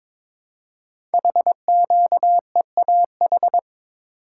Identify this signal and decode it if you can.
Morse code, heard as a keyed tone, about 22 wpm, 700 Hz. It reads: HQEAH
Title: Code Groups - 5 Letters - 22wpm